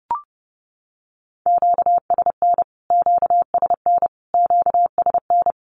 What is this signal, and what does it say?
Morse code, heard as a keyed tone, about 30 wpm, 700 Hz. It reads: QSN QSN QSN
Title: Q-Codes Repeated 3x 30wpm